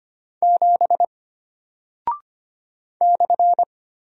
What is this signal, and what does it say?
Morse code, heard as a keyed tone, about 25 wpm, 700 Hz. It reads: 7 /
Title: Single Character (Letter:Number:Punctuation:Prosign) Alt 25wpm